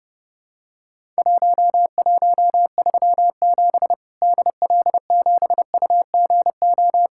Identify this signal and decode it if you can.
Morse code, heard as a keyed tone, about 32 wpm, 700 Hz. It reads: 1137 DL7UGO
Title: Worked All Europe DX Contest - QTC Exchange - 30wpm